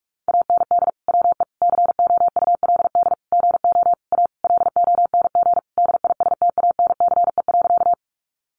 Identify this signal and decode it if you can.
Morse code, heard as a keyed tone, about 44 wpm, 700 Hz. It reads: AND WE COULD GO A LONG DISTANCE.
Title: Sentences from Top 1200 Words 45wpm